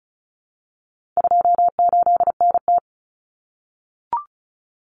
Morse code, heard as a keyed tone, about 34 wpm, 700 Hz. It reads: 28NT
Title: Common Cut Numbers - 4-Digit 35wpm